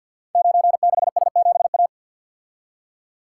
Morse code, heard as a keyed tone, about 48 wpm, 700 Hz. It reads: 96S7A
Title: Code Groups - 5 Letter:Numbers - 50wpm